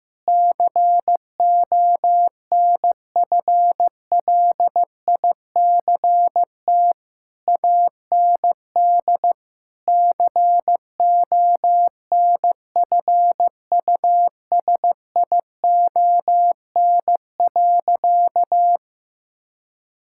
Morse code, heard as a keyed tone, about 15 wpm, 700 Hz. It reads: CONFLICT AND CONFUSION.
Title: Sentences from Top 1600 Words 15wpm